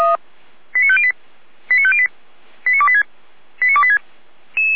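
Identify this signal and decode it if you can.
Five-tone selective call (ZVEI), heard as quick runs of five tones, tones of about 70 ms. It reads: 89508 89508 89286 89286
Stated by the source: ZVEI five-tone selective call received by SDR.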